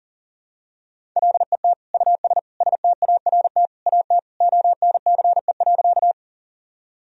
Morse code, heard as a keyed tone, about 40 wpm, 700 Hz. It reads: LET US START AT ONCE.
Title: Sentences from Top 1000 Words - Review Part 1 - 40wpm